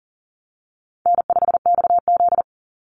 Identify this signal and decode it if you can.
Morse code, heard as a keyed tone, about 40 wpm, 700 Hz. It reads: N5XZ
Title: Easy Call Signs Repeated 3x - 40wpm